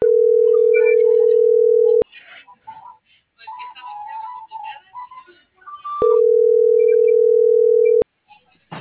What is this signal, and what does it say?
Ringback tone (North America), 440 + 480 Hz.